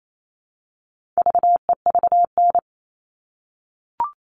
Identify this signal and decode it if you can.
Morse code, heard as a keyed tone, about 28 wpm, 700 Hz. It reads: VEVN
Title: All Cut Numbers - 4-Digit 28wpm